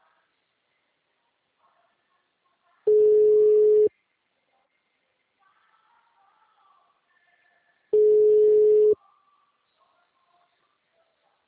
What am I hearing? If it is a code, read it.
Ringback tone (Europe, 425 Hz).